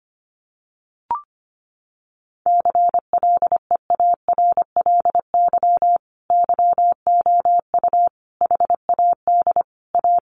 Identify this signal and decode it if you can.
Morse code, heard as a keyed tone, about 25 wpm, 700 Hz. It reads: CLEARLY YOU HAD A
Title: Sentences from Top 1300 Words 25wpm